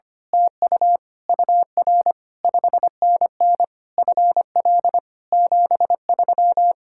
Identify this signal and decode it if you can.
Morse code, heard as a keyed tone, about 24 wpm, 700 Hz. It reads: TU UR 5NN FL 73
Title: Speed-Racing Common QSO Phrases Encore - 25wpm